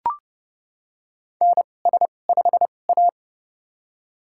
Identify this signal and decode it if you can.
Morse code, heard as a keyed tone, about 30 wpm, spaced as for 22 wpm, 700 Hz. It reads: NS5A
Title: USA Call Signs - 4 Characters - 2x Character Spacing - 30wpm